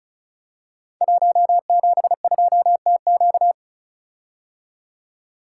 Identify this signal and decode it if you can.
Morse code, heard as a keyed tone, about 36 wpm, 690 Hz. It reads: 172TQ
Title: Code Groups - 5 Letter:Numbers - 35wpm